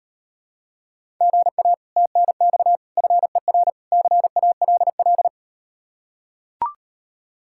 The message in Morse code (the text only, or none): GA TNX FER CALL